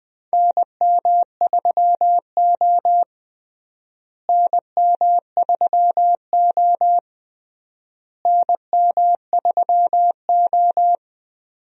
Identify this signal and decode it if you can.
Morse code, heard as a keyed tone, about 20 wpm, 700 Hz. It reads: NM3O NM3O NM3O
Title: USA Call Signs - 4 Characters - Repeated 3x - 20wpm